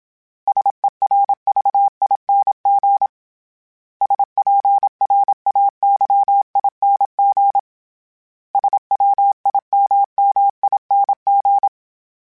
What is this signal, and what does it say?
Morse code, heard as a keyed tone, about 27 wpm, 800 Hz. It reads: SERVING SPRAYING SWIMMING
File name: Special INGs-mp3s / Special INGs-mp3s / Sngl Ing @25